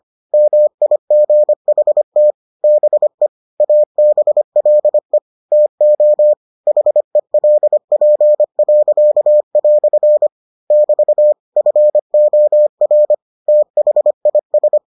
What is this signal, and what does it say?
Morse code, heard as a keyed tone, about 25 wpm, 600 Hz. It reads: MIGHT BE ABLE TO HELP." <BT> FOR THIS